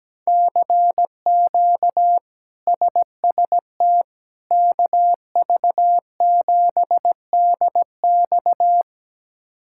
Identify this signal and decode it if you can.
Morse code, heard as a keyed tone, about 17 wpm, 700 Hz. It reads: CQ SST KV7DX
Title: K1USN SST QSO Practice - 17wpm